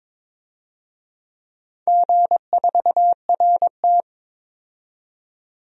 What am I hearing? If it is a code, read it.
Morse code, heard as a keyed tone, about 22 wpm, 700 Hz. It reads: G4RT
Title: DXCC-100 Call Signs - 4 Characters - 22wpm